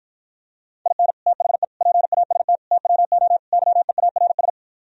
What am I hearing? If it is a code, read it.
Morse code, heard as a keyed tone, about 53 wpm, 700 Hz. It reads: IN THE PAST TWO YEARS